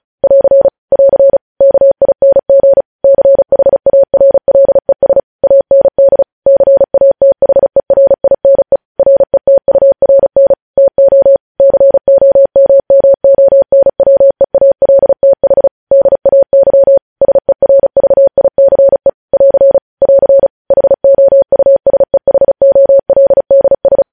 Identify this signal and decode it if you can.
Morse code, heard as a keyed tone, about 36 wpm, 550 Hz. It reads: <AR> <AR> KING CHARLES AND CATHERINE RETURN TO COMMONWEALTH DAY SERVICE <AR> <AR> HOUSEHOLDS
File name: podcast / News headlines-1x-35x35